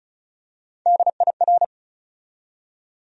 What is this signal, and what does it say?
Morse code, heard as a keyed tone, about 36 wpm, 700 Hz. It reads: DIR
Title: USA Call Sign Postfixes - 3 Letters 35wpm